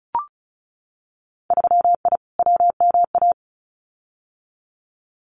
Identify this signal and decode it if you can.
Morse code, heard as a keyed tone, about 36 wpm, 700 Hz. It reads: 3I WMA
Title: Winter Field Day Exchanges 35wpm